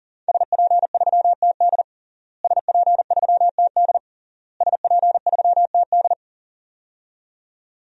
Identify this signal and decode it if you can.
Morse code, heard as a keyed tone, about 40 wpm, 700 Hz. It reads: SP3TD SP3TD SP3TD
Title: DXCC-100 Call Signs - 5 Characters - Repeated 3x - 40wpm